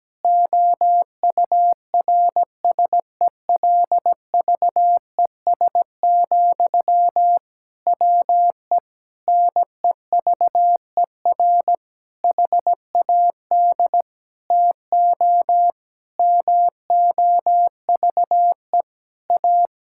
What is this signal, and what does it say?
Morse code, heard as a keyed tone, about 17 wpm, 700 Hz. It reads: OURSELVES, WE NEVER HAD TO MOVE A
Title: Sentences from Top 1600 Words 17wpm